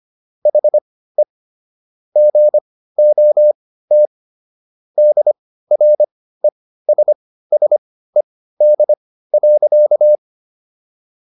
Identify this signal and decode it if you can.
Morse code, heard as a keyed tone, about 25 wpm, spaced as for 15 wpm, 600 Hz. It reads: HE GOT DRESSED.